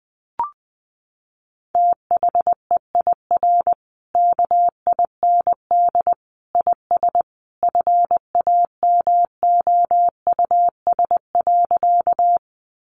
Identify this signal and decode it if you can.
Morse code, heard as a keyed tone, about 20 wpm, 700 Hz. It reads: THEIR KIND IS FAMOUS.
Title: Sentences from Top 1500 Words 20wpm